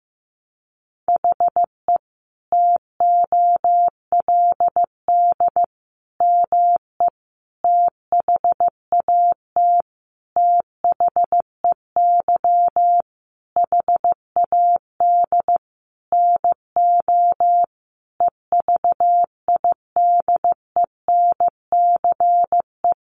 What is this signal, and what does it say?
Morse code, heard as a keyed tone, about 15 wpm, 700 Hz. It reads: HE TOLD ME THAT THEY HAD NO EVIDENCE
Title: Sentences from Top 1300 Words 15wpm